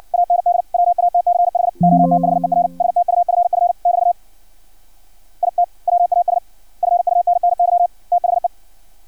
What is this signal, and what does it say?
Morse code, heard as a keyed tone, about 56 wpm, 700 Hz. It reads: AND WATCH OTHER STUFF <BT> IT WAS FUNNY THE
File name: archived / CRSnet